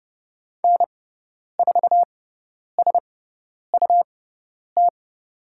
Morse code, heard as a keyed tone, about 30 wpm, spaced as for 10 wpm, 700 Hz. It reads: N4SUT
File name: Lesson 17g - Letter u - Call Signs 30wpm